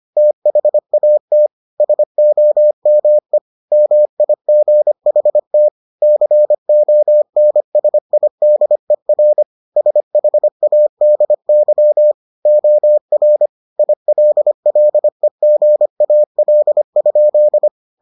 Morse code, heard as a keyed tone, about 25 wpm, 600 Hz. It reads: THAT SOME MIGHT CONSIDER SHADY OR ILLEGAL?